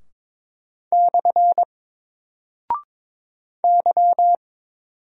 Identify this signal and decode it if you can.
Morse code, heard as a keyed tone, about 22 wpm, 700 Hz. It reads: / Y
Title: Single Character (Letter:Number:Punctuation:Prosign) Alt 22wpm